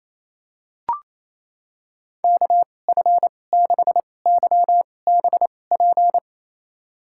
Morse code, heard as a keyed tone, about 27 wpm, spaced as for 20 wpm, 700 Hz. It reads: KF6YBP
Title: USA Call Signs - 2x Character Spacing - 28wpm